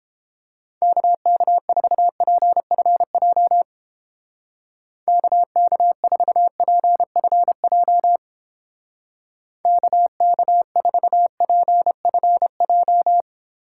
Morse code, heard as a keyed tone, about 29 wpm, 700 Hz. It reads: KK4PFJ KK4PFJ KK4PFJ